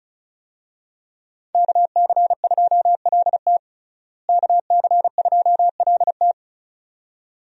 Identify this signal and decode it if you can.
Morse code, heard as a keyed tone, about 34 wpm, 700 Hz. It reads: KC2LT KC2LT
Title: USA Call Signs - 5 Characters - Repeated 2x - 35wpm